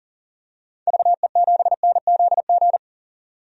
Morse code, heard as a keyed tone, about 40 wpm, 700 Hz. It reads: VE7NZG
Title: DXCC-100 Call Signs - 6 Characters - 40wpm